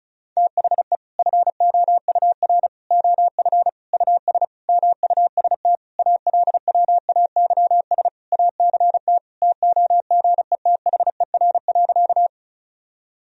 Morse code, heard as a keyed tone, about 36 wpm, 700 Hz. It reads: THE FOUR OF US MUST ALWAYS ACT TOGETHER.
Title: Sentences from Top 1000 Words - Review Part 1 - 35wpm